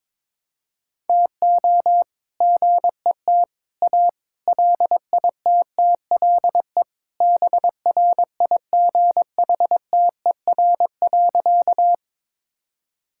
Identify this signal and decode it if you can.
Morse code, heard as a keyed tone, about 22 wpm, 700 Hz. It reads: TO GET A LITTLE BRIGHTER.
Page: morse-code-ninja-practice-22wpm